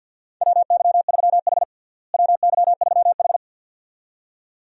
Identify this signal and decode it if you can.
Morse code, heard as a keyed tone, about 52 wpm, 700 Hz. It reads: WY3H WY3H
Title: USA Call Signs - 4 Characters - Repeated 2x - 50wpm